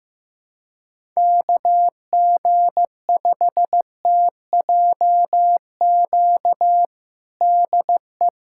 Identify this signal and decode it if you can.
Morse code, heard as a keyed tone, about 15 wpm, 700 Hz. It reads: KG5TJQ DE
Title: USA Callsign DE Callsign - 6 Characters - 15wpm